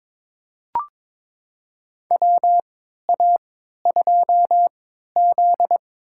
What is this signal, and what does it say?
Morse code, heard as a keyed tone, about 22 wpm, spaced as for 13 wpm, 700 Hz. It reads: WA2Z
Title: USA Call Signs - 4 Characters - 3x Character Spacing - 22wpm